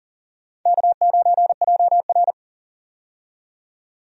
Morse code, heard as a keyed tone, about 40 wpm, 700 Hz. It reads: K9JR
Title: USA Call Signs - 4 Characters - 40wpm